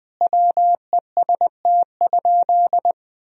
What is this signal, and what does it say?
Morse code, heard as a keyed tone, about 20 wpm, 700 Hz. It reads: WEST?
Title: Sentences from Top 1000 Words - Review Part 2 - 20wpm